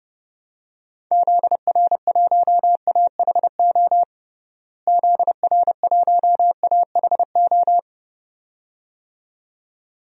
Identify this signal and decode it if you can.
Morse code, heard as a keyed tone, about 30 wpm, 700 Hz. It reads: ZR1AHO ZR1AHO
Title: DXCC-100 Call Signs - 6 Characters - Repeated 2x - 30wpm